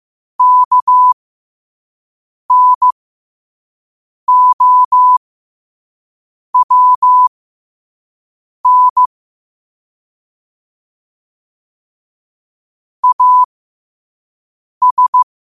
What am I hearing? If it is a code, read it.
Morse code, heard as a keyed tone, about 15 wpm, spaced as for 5 wpm, 1000 Hz. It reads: KNOWN AS